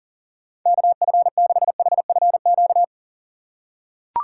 Morse code, heard as a keyed tone, about 37 wpm, 700 Hz. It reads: KF6HFQ
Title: USA Call Signs - 6 Characters - 3x Character Spacing - 40wpm